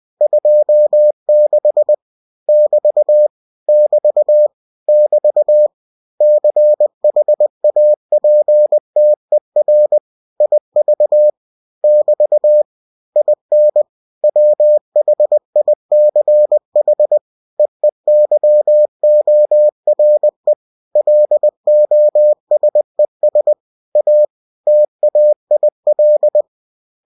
Morse code, heard as a keyed tone, about 20 wpm, 600 Hz. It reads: 26 <BT> <BT> <BT> CHAPTER IV <BT> IN WHICH EEYORE LOSES A TAIL
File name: cw-books / winnie-the-pooh / winnie-20wpm-page-0026